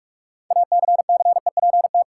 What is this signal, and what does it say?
Morse code, heard as a keyed tone, about 46 wpm, 700 Hz. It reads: ACCEPT
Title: Sentences from Top 1300 Words - New Word Drill Alt - 45wpm